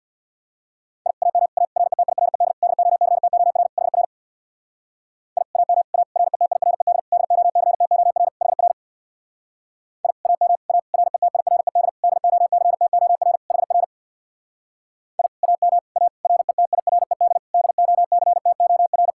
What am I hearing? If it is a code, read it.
Morse code, heard as a keyed tone, about 67 wpm, 700 Hz. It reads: I AM A RETIRED DOCTOR HR I AM A RETIRED DOCTOR HR I AM A RETIRED DOCTOR HR I AM A RETIRED DOCTOR